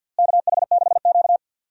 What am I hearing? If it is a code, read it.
Morse code, heard as a keyed tone, about 45 wpm, 700 Hz. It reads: KH6Q
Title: DXCC-100 Call Signs - 4 Characters - Repeated 3x - 50wpm